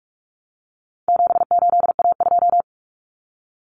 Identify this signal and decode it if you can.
Morse code, heard as a keyed tone, about 44 wpm, 700 Hz. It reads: 78A2
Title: Common Cut Numbers - 4-Digit 45wpm